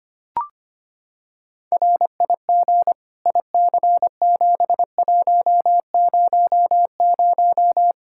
Morse code, heard as a keyed tone, about 25 wpm, 700 Hz. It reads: RIG IC7100
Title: K1USN SST QSO Practice 25wpm